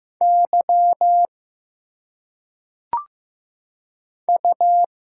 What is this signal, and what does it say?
Morse code, heard as a keyed tone, about 15 wpm, 700 Hz. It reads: Y U
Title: Single Letters - Phonetic Pronunciation 15wpm